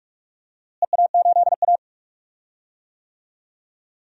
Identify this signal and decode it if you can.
Morse code, heard as a keyed tone, about 43 wpm, 700 Hz. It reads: EA8A
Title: DXCC-100 Call Signs - 4 Characters - Repeated 3x - 45wpm